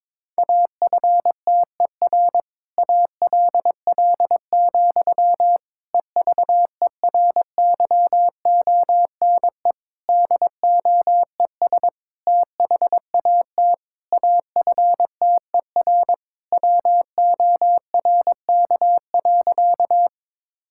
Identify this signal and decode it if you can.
Morse code, heard as a keyed tone, about 22 wpm, 700 Hz. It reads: AFTER ALL, EVERYONE DOES THAT AFTER WORK.
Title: Sentences from Top 1000 Words - Review Part 5 - 22wpm